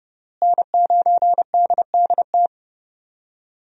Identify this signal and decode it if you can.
Morse code, heard as a keyed tone, about 32 wpm, 700 Hz. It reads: N9DDT
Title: USA Call Signs - 5 Characters - Repeated 3x - 30wpm